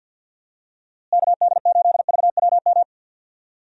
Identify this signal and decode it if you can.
Morse code, heard as a keyed tone, about 52 wpm, 700 Hz. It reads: KD8VWM
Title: USA Call Signs - 6 Characters - Repeated 2x - 50wpm